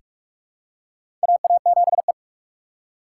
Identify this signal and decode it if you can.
Morse code, heard as a keyed tone, about 43 wpm, 700 Hz. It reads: AA7E
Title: USA Call Signs - 3x Character Spacing - 45wpm